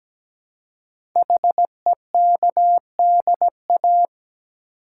Morse code, heard as a keyed tone, about 17 wpm, 700 Hz. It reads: HEKDA
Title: Code Groups - 5 Letters - 17wpm